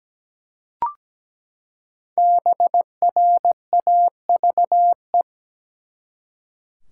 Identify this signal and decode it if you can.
Morse code, heard as a keyed tone, about 17 wpm, 700 Hz. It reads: BRAVE
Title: Five Letter Words - 17wpm